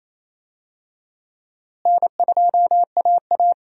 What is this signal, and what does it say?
Morse code, heard as a keyed tone, about 28 wpm, 700 Hz. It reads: N2AA
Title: USA Call Signs - 4 Characters - Repeated 3x - 28wpm